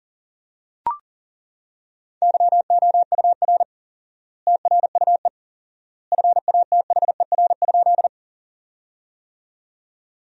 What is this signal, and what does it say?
Morse code, heard as a keyed tone, about 40 wpm, 700 Hz. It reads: YOUR TRUE FATHER?
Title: 4x Word-Spacing - Sentences from Top 400 Words - 40wpm